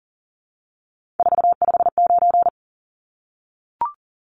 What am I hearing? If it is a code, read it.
Morse code, heard as a keyed tone, about 40 wpm, 700 Hz. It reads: 459
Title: Numbers - 3-Digit 40wpm